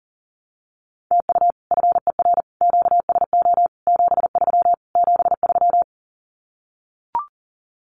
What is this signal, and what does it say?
Morse code, heard as a keyed tone, about 40 wpm, 700 Hz. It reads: TU FER QSO 73 73
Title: Common QSO Phrases Encore 40wpm